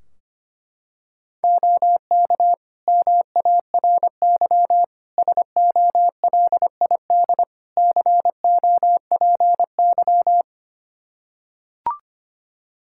Morse code, heard as a keyed tone, about 25 wpm, 700 Hz. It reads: OK MARY SOLID COPY
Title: Practice QSO - Exchange 3 - More Info - 25wpm